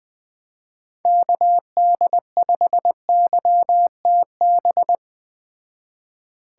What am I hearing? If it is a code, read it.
Morse code, heard as a keyed tone, about 20 wpm, 700 Hz. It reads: KD5YTB